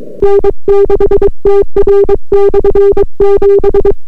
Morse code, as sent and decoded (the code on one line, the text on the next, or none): -. -.... - .-. -..-. --...
N6TR/7